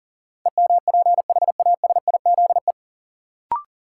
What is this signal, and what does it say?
Morse code, heard as a keyed tone, about 40 wpm, 700 Hz. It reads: EMPHASIZE
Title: Sentences from Top 1500 Words - New Word Drill - 40wpm